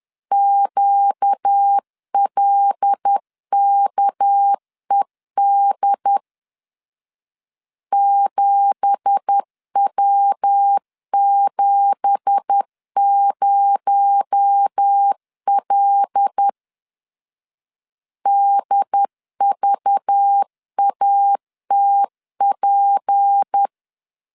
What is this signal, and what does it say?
Morse code, heard as a keyed tone, about 10 wpm, 800 Hz. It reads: QLKED 7W70L DVATP
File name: LEZIONI / 9 2867 10wpm 2733